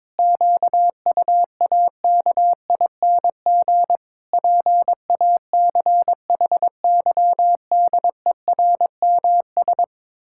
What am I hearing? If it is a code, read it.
Morse code, heard as a keyed tone, about 22 wpm, 700 Hz. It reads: QUAKING PACHYDERMS
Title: Pangram Challenge v2 - 22wpm